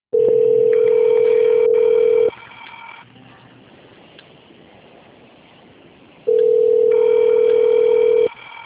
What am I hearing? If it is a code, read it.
Ringback tone (North America), 440 + 480 Hz.
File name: out- - - - - .25861